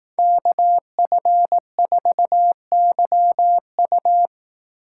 Morse code, heard as a keyed tone, about 19 wpm, 700 Hz. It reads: KF4YU
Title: Speed Racing - USA 2x2 Call Signs 15wpm